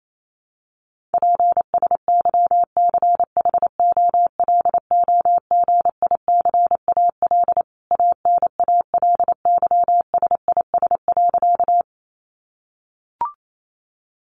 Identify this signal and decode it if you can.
Morse code, heard as a keyed tone, about 28 wpm, 700 Hz. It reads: PSYCHOLOGICAL ANALYSIS.
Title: Sentences from Top 1400 Words 28wpm